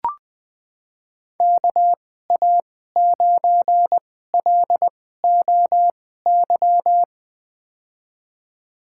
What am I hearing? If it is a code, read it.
Morse code, heard as a keyed tone, about 21 wpm, spaced as for 15 wpm, 700 Hz. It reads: KA9LOY
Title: USA Call Signs - 6 Characters - 2x Character Spacing - 20wpm